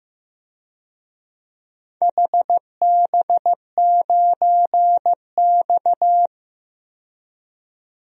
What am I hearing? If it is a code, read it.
Morse code, heard as a keyed tone, about 15 wpm, 700 Hz. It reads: HB9X
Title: DXCC-100 Call Signs - 4 Characters - 15wpm